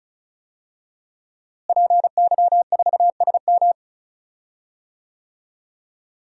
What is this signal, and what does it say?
Morse code, heard as a keyed tone, about 36 wpm, 700 Hz. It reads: PY4SM
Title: DXCC-100 Call Signs - 5 Characters - Repeated 3x - 35wpm